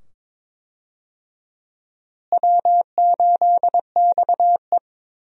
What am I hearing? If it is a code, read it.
Morse code, heard as a keyed tone, about 22 wpm, 700 Hz. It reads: W8XE
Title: USA Call Signs - 4 Characters - Repeated 2x - 22wpm